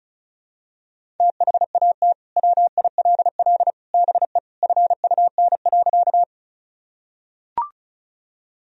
Morse code, heard as a keyed tone, about 36 wpm, 700 Hz. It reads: THAT WILL BE FUN.
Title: Sentences from Top 1000 Words - Review Part 4 - 35wpm